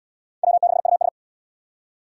Morse code, heard as a keyed tone, about 68 wpm, 700 Hz. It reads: W6US